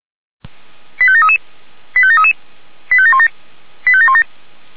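Five-tone selective call (ZVEI), heard as quick runs of five tones, tones of about 70 ms. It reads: 85733 85733 85717 85717
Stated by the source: ZVEI five-tone selective call 89484 received by SDR.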